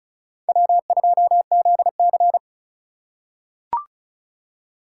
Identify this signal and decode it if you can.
Morse code, heard as a keyed tone, about 36 wpm, 690 Hz. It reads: W2ZC
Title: USA Call Signs - 4 Characters - 3x Character Spacing - 35wpm